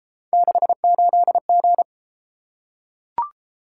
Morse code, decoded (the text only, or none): 68G